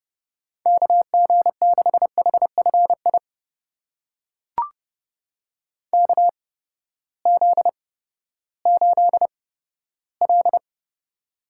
Morse code, heard as a keyed tone, about 30 wpm, spaced as for 15 wpm, 700 Hz. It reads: KG6HFI KZ8L
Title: USA Call Signs - 8x Character Spacing - 30wpm